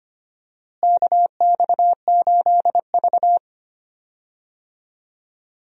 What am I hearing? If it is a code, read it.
Morse code, heard as a keyed tone, about 25 wpm, 700 Hz. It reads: KX8V
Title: USA Call Signs - 4 Characters - Repeated 2x - 25wpm